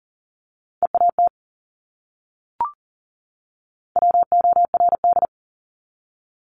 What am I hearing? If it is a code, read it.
Morse code, heard as a keyed tone, about 40 wpm, 700 Hz. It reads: EAT WORD
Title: Top 500 Words 40wpm